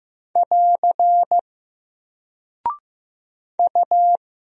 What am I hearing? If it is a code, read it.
Morse code, heard as a keyed tone, about 15 wpm, 700 Hz. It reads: <AR> U
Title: Single Character (Letter:Number:Punctuation:Prosign) 15wpm